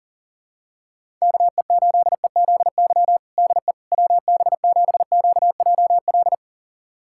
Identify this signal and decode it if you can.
Morse code, heard as a keyed tone, about 40 wpm, 700 Hz. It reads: KE8EZY DE WB7QJL
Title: USA Callsign DE Callsign - 6 Characters - 40wpm